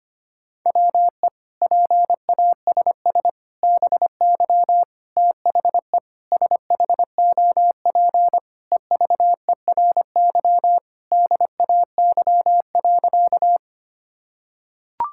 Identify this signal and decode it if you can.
Morse code, heard as a keyed tone, about 25 wpm, 700 Hz. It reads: WE PASS BY THE SHOP EVERY DAY.
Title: Sentences from Top 1600 Words 25wpm